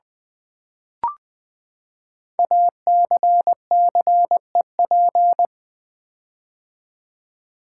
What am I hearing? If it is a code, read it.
Morse code, heard as a keyed tone, about 20 wpm, 700 Hz. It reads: ACCEP
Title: Build A Word - 5 Letters At A Time - 10-Letter Words - 20wpm